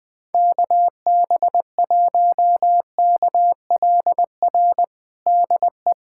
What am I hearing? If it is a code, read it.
Morse code, heard as a keyed tone, about 21 wpm, 700 Hz. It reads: KB1KLR DE
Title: USA Callsign DE Callsign - 6 Characters - 20wpm